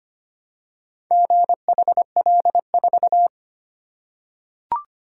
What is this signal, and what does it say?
Morse code, heard as a keyed tone, about 25 wpm, 700 Hz. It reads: GHL4
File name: practice / 25wpm / Four Letter-Number Combinations - 25wpm